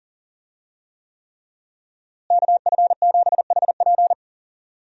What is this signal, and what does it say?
Morse code, heard as a keyed tone, about 40 wpm, 700 Hz. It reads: KF7HP
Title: USA Call Signs - 5 Characters - Repeated 3x - 40wpm